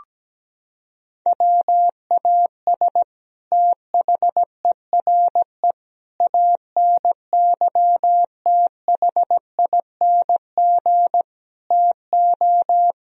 Morse code, heard as a keyed tone, about 17 wpm, 700 Hz. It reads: WAS THERE ANYTHING TO
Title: Sentences from Top 1000 Words - Review Part 1 - 17wpm